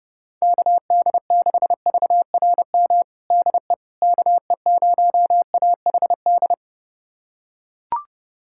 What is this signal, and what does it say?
Morse code, heard as a keyed tone, about 30 wpm, 700 Hz. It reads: KD6VRM DE KE0AHD
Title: USA Callsign DE Callsign 30wpm